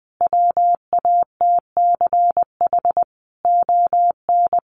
Morse code, heard as a keyed tone, about 20 wpm, 700 Hz. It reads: WATCH ON
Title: Sentences from Top 1300 Words 20wpm